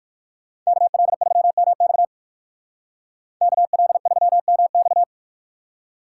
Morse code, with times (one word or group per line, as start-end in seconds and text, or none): 0.7-2.0 KL3MX
3.4-5.0 KL3MX